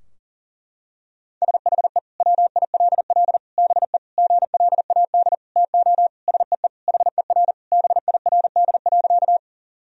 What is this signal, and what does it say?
Morse code, heard as a keyed tone, about 37 wpm, 700 Hz. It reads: SHE WILL BE GLAD TO SEE HER BIRD.
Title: Sentences from Top 1000 Words - Review Part 2 - 40wpm